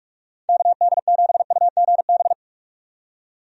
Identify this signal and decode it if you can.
Morse code, heard as a keyed tone, about 46 wpm, 700 Hz. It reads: KD7UGB
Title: USA Call Signs - 6 Characters - 45wpm